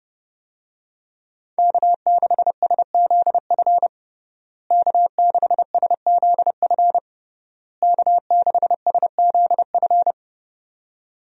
Morse code, heard as a keyed tone, about 31 wpm, 700 Hz. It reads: K6SZF K6SZF K6SZF
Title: USA Call Signs - 5 Characters - Repeated 3x - 30wpm